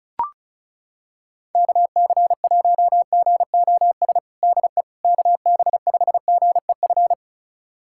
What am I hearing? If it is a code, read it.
Morse code, heard as a keyed tone, about 36 wpm, 700 Hz. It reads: KC1GOS DE KB5GEF
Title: USA Callsign DE Callsign - 6 Characters - 35wpm